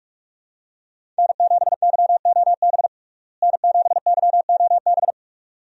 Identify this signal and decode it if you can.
Morse code, heard as a keyed tone, about 46 wpm, 700 Hz. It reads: N7YOB N7YOB
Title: USA Call Signs - 5 Characters - Repeated 2x - 45wpm